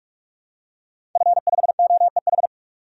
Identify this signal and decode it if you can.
Morse code, heard as a keyed tone, about 44 wpm, 700 Hz. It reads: F5OEH
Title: DXCC-100 Call Signs - 5 Characters - 45wpm